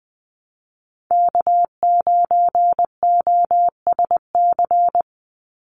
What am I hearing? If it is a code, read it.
Morse code, heard as a keyed tone, about 19 wpm, 690 Hz. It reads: K9OSC
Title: NA SOTA Chaser Callsigns 20wpm